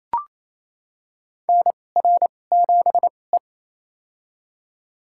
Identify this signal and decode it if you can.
Morse code, heard as a keyed tone, about 28 wpm, spaced as for 20 wpm, 700 Hz. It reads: NR7E
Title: USA Call Signs - 4 Characters - 2x Character Spacing - 28wpm